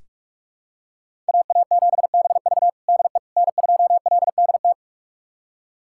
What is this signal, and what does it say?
Morse code, heard as a keyed tone, about 46 wpm, 700 Hz. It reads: AA7BV DE N2LDT
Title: USA Callsign DE Callsign - 5 Characters - 45wpm